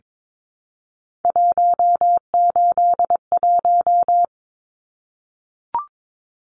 Morse code, heard as a keyed tone, about 22 wpm, 700 Hz. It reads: 181
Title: Numbers - 3-Digit 22wpm